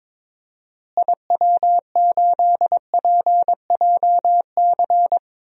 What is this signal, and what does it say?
Morse code, heard as a keyed tone, about 22 wpm, 700 Hz. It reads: IW8PJC
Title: DXCC-100 Call Signs - 6 Characters - Repeated 2x - 22wpm